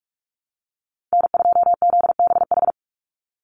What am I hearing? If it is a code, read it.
Morse code, heard as a keyed tone, about 44 wpm, 700 Hz. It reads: N2ZBH
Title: USA Call Signs 45wpm